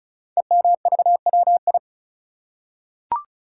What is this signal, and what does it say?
Morse code, heard as a keyed tone, about 36 wpm, 700 Hz. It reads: EMVWI
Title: Code Groups - 5 Letters - 35wpm